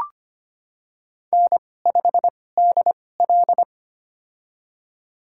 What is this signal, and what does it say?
Morse code, heard as a keyed tone, about 25 wpm, spaced as for 18 wpm, 700 Hz. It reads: N5DL
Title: USA Call Signs - 2x Character Spacing - 25wpm